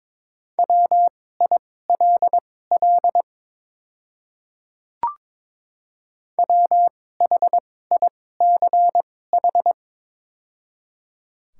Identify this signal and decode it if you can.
Morse code, heard as a keyed tone, about 22 wpm, spaced as for 16 wpm, 700 Hz. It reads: WILL WHICH
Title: 2x Character Spacing - Top 100 Words - 22wpm